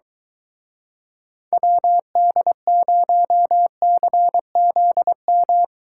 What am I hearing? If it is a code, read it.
Morse code, heard as a keyed tone, about 23 wpm, 700 Hz. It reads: WD0CZM